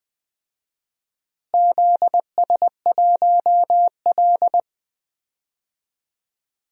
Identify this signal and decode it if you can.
Morse code, heard as a keyed tone, about 20 wpm, 700 Hz. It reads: ZS1L
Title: DXCC-100 Call Signs - 4 Characters - Repeated 2x - 20wpm